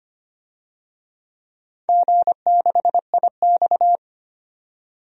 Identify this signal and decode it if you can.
Morse code, heard as a keyed tone, about 25 wpm, 700 Hz. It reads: G6IX
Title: DXCC-100 Call Signs - 4 Characters - 25wpm